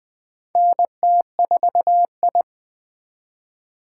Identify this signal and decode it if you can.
Morse code, heard as a keyed tone, about 20 wpm, 700 Hz. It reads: NT4I
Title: USA Call Signs - 4 Characters - 2x Character Spacing - 20wpm